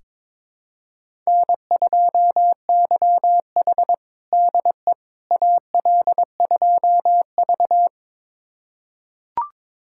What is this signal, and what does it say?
Morse code, heard as a keyed tone, about 23 wpm, 700 Hz. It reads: N2YH DE AL2V
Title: USA Callsign DE Callsign - 4 Characters - 22wpm